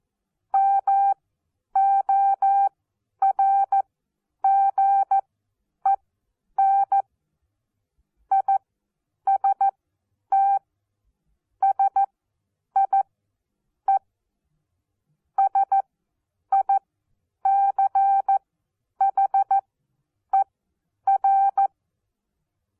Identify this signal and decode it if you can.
Morse code, heard as a keyed tone, about 14 wpm, spaced as for 9 wpm, 790 Hz. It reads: MORGEN IST SIE SICHER
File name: Onlinelehrgang / morsen / DJ4UF Morsen14